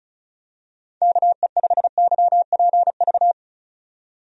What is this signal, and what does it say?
Morse code, heard as a keyed tone, about 36 wpm, 700 Hz. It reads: KE5YPV
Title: USA Call Signs - 6 Characters - 35wpm